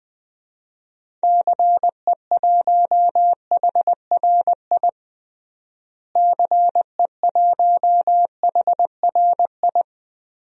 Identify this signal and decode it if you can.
Morse code, heard as a keyed tone, about 20 wpm, 700 Hz. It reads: CE1HRI CE1HRI
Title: DXCC-100 Call Signs - 6 Characters - Repeated 2x - 20wpm